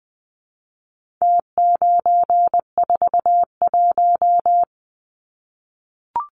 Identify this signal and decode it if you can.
Morse code, heard as a keyed tone, about 20 wpm, 700 Hz. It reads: T941
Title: Numbers - Contest Style - 4-Digit 20wpm